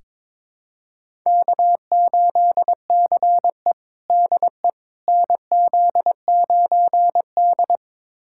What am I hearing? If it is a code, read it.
Morse code, heard as a keyed tone, about 22 wpm, 700 Hz. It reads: K8CE DE NZ9D
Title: USA Callsign DE Callsign - 4 Characters - 22wpm